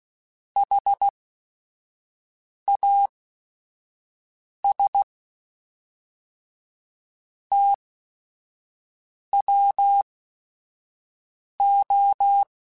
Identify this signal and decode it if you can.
Morse code, heard as a keyed tone, about 16 wpm, spaced as for 5 wpm, 800 Hz. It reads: HAS TWO
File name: Technical Subjects / Technical Subjects-mp3s / XC @5